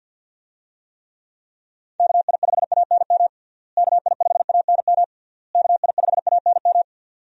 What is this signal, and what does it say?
Morse code, heard as a keyed tone, about 47 wpm, 700 Hz. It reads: KI5ANM KI5ANM KI5ANM
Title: USA Call Signs - 6 Characters - Repeated 3x - 50wpm